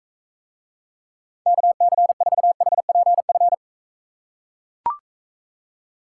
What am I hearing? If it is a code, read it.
Morse code, heard as a keyed tone, about 42 wpm, 700 Hz. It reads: KC4HPF